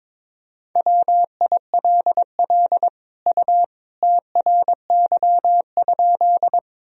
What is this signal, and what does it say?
Morse code, heard as a keyed tone, about 21 wpm, 700 Hz. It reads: WILL U TRY?
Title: Sentences from Top 200 Words and Abbreviations 22wpm